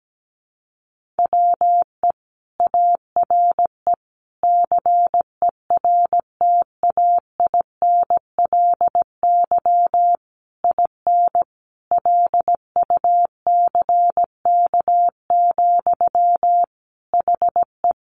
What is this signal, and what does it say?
Morse code, heard as a keyed tone, about 17 wpm, 700 Hz. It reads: WE ARE CERTAINLY IN LUCK, HE
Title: Sentences from Top 1500 Words 17wpm